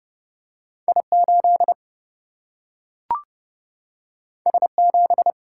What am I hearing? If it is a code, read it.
Morse code, heard as a keyed tone, about 30 wpm, 700 Hz. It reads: I 8 S 7
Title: Single-Letter Single-Number 30wpm